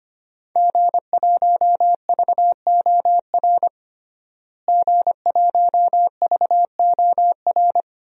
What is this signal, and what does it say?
Morse code, heard as a keyed tone, about 25 wpm, 700 Hz. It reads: G1VOR G1VOR
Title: DXCC-100 Call Signs - 5 Characters - Repeated 2x - 25wpm